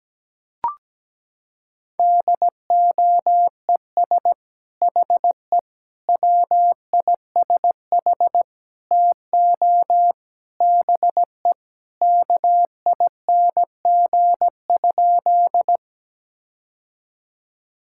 Morse code, heard as a keyed tone, about 17 wpm, 700 Hz. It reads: DOES HE WISH TO BE KING?
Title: Sentences from Top 1600 Words 17wpm